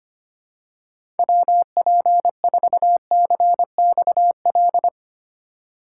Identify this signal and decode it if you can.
Morse code, heard as a keyed tone, about 25 wpm, 700 Hz. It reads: WP4CXL
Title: USA Call Signs - 6 Characters - 2x Character Spacing - 25wpm